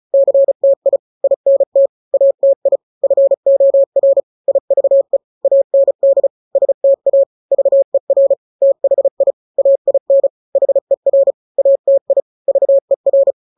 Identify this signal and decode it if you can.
Morse code, heard as a keyed tone, about 34 wpm, 550 Hz. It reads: CTI INT ATI FOR IVE AND STA VER THI AIN HER ATI VER
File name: 3-grams 35 30 WPM